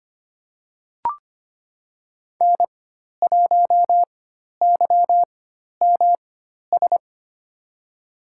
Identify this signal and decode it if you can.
Morse code, heard as a keyed tone, about 25 wpm, spaced as for 12 wpm, 700 Hz. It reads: N1YMS
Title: USA Call Signs - 5 Characters - 4x Character Spacing - 25wpm